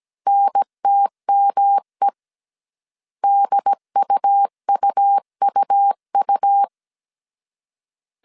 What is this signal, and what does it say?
Morse code, heard as a keyed tone, about 17 wpm, 800 Hz. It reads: NTME DUUUU